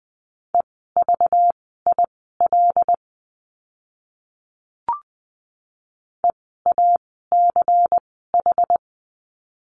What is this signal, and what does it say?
Morse code, heard as a keyed tone, about 20 wpm, spaced as for 15 wpm, 700 Hz. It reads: EVIL EACH
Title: 192 Lesson 31f - Letter v - Words 20wpm